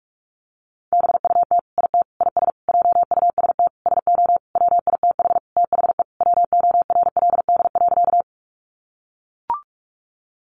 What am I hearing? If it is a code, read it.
Morse code, heard as a keyed tone, about 44 wpm, 700 Hz. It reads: BUT IT IS JUST SO WITH THE WORLD.
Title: Sentences from Top 200 Words 45wpm